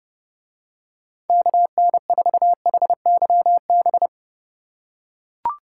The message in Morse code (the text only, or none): KN4HYB